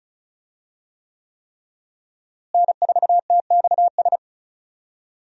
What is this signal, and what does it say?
Morse code, heard as a keyed tone, about 36 wpm, 700 Hz. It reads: N4TXS
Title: USA Call Signs - 5 Characters - Repeated 3x - 35wpm